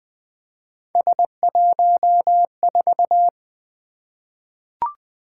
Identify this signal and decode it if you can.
Morse code, heard as a keyed tone, about 20 wpm, 700 Hz. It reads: S14
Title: Three Letter-Number Combinations - 20wpm